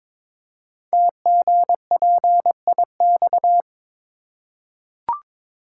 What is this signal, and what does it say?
Morse code, heard as a keyed tone, about 23 wpm, 700 Hz. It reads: TGPIX
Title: Code Groups - 5 Letters - 22wpm